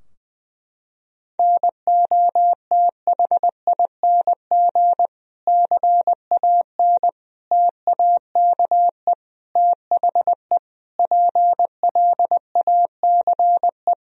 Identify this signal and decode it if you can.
Morse code, heard as a keyed tone, about 20 wpm, 700 Hz. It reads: NOTHING CAN TAKE THE PLACE
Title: Sentences from Top 1000 Words - Review Part 4 - 20wpm